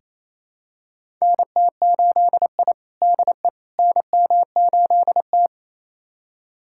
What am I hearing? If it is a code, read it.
Morse code, heard as a keyed tone, about 27 wpm, 700 Hz. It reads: NT8I DE NM8T
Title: USA Callsign DE Callsign - 4 Characters - 28wpm